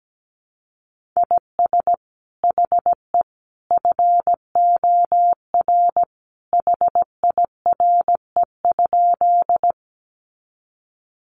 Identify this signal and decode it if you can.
Morse code, heard as a keyed tone, about 17 wpm, 700 Hz. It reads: IS HE FOR HIRE?
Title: Sentences from Top 1400 Words 17wpm